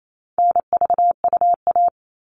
Morse code, heard as a keyed tone, about 28 wpm, 700 Hz. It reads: NVUA
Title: All Cut Numbers - 4-Digit 28wpm